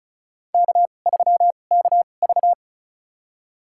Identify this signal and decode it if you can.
Morse code, heard as a keyed tone, about 36 wpm, spaced as for 25 wpm, 700 Hz. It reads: K3KV
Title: USA Call Signs - 4 Characters - 2x Character Spacing - 35wpm